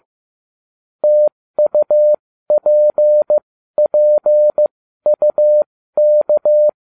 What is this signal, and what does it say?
Morse code, heard as a keyed tone, about 15 wpm, spaced as for 13 wpm, 600 Hz. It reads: TUPPUK